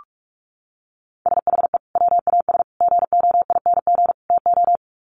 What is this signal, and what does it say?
Morse code, heard as a keyed tone, about 44 wpm, 700 Hz. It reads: SHE WAS GOING TO
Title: Sets of 4 Words 45wpm